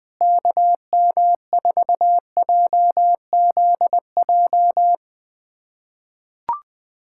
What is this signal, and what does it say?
Morse code, heard as a keyed tone, about 20 wpm, 700 Hz. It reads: KM4JZJ
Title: USA Callsign DE Callsign - 6 Characters - 20wpm